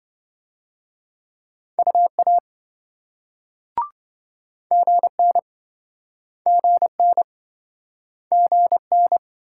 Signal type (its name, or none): Morse code